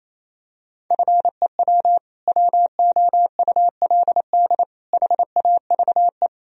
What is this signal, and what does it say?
Morse code, heard as a keyed tone, about 28 wpm, 700 Hz. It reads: FEW WOULD HAVE
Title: Sentences from Top 1000 Words - Review Part 1 - 28wpm